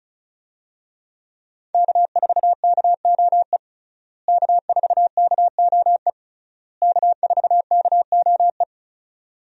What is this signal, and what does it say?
Morse code, heard as a keyed tone, about 34 wpm, 700 Hz. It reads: K4KOE K4KOE K4KOE
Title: USA Call Signs - 5 Characters - Repeated 3x - 35wpm